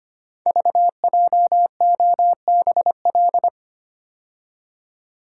Morse code, heard as a keyed tone, about 25 wpm, 700 Hz. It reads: VJOBL
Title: Code Groups - 5 Letters - 25wpm